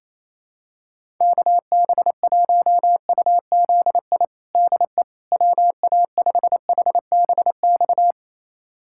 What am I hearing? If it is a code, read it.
Morse code, heard as a keyed tone, about 28 wpm, 700 Hz. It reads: KB1UZI DE WA5HBX
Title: USA Callsign DE Callsign 28wpm